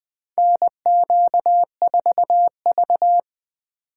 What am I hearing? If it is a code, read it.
Morse code, heard as a keyed tone, about 20 wpm, 700 Hz. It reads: NQ4V
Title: USA Call Signs - 4 Characters - 2x Character Spacing - 20wpm